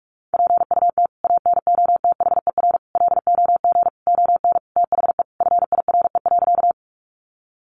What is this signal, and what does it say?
Morse code, heard as a keyed tone, about 44 wpm, 700 Hz. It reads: PUT ANOTHER LOG ON THE FIRE.
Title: Sentences from Top 1400 Words 45wpm